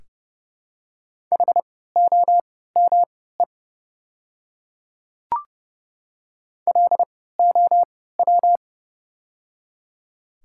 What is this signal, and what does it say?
Morse code, heard as a keyed tone, about 30 wpm, spaced as for 17 wpm, 700 Hz. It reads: HOME LOW
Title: 3x Character Spacing - Top 300 Words - 30wpm